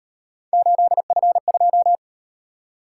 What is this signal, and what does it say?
Morse code, heard as a keyed tone, about 38 wpm, 700 Hz. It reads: 8F2